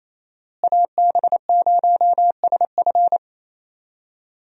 Morse code, heard as a keyed tone, about 27 wpm, 700 Hz. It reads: AB0SF
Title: USA Call Signs - 5 Characters - Repeated 3x - 28wpm